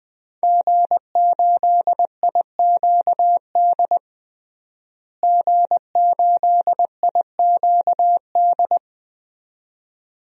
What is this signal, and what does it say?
Morse code, heard as a keyed tone, about 20 wpm, 700 Hz. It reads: G8IQD G8IQD
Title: DXCC-100 Call Signs - 5 Characters - Repeated 2x - 20wpm